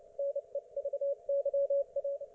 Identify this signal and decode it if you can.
Morse code, heard as a keyed tone, about 29 wpm, 560 Hz. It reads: NEVYA